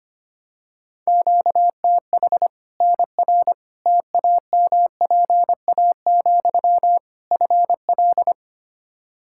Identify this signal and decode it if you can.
Morse code, heard as a keyed tone, about 25 wpm, 700 Hz. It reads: QTH NR TAMPA, FL
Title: Practice QSO - Exchange 1 - Beginning - 25wpm